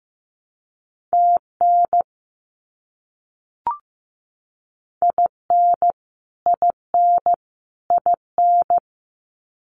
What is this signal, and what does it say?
Morse code, heard as a keyed tone, about 15 wpm, 700 Hz. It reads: TN IN IN IN
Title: US States Repeated 3x 15wpm